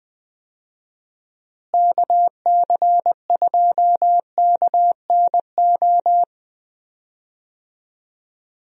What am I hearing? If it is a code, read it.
Morse code, heard as a keyed tone, about 20 wpm, 700 Hz. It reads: KC2KNO
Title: USA Call Signs - 6 Characters - Repeated 2x - 20wpm